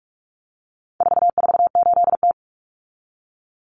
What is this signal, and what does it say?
Morse code, heard as a keyed tone, about 44 wpm, 700 Hz. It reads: 448T
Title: Common Cut Numbers - 4-Digit 45wpm